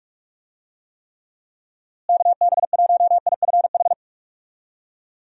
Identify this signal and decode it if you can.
Morse code, heard as a keyed tone, about 46 wpm, 700 Hz. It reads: KB1IFH
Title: USA Call Signs - 6 Characters - Repeated 3x - 45wpm